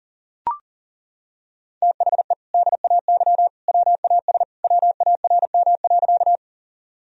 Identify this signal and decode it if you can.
Morse code, heard as a keyed tone, about 40 wpm, 700 Hz. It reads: THE DAY WAS WARM.
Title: Sentences from Top 1000 Words - Review Part 2 - 40wpm